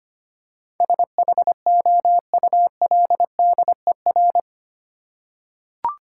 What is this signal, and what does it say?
Morse code, heard as a keyed tone, about 25 wpm, 700 Hz. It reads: SHOULDER
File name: 8 Letter Words - First 3 Letters Given - 25wpm